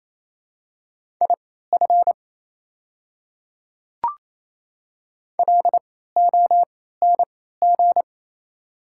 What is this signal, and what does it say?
Morse code, heard as a keyed tone, about 28 wpm, spaced as for 16 wpm, 700 Hz. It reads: IF LONG
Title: 3x Character Spacing - Top 100 Words - 28wpm